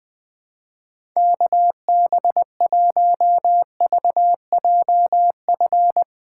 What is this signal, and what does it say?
Morse code, heard as a keyed tone, about 21 wpm, 700 Hz. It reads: KB1VJF
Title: USA Call Signs - 6 Characters - 4x Character Spacing - 20wpm